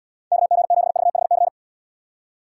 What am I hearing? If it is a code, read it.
Morse code, heard as a keyed tone, about 69 wpm, 700 Hz. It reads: KG7VHC